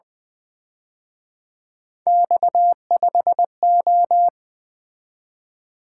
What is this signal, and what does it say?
Morse code, heard as a keyed tone, about 24 wpm, 700 Hz. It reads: X5O
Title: Speed-Racing - Three Letter-Number Combinations - 20wpm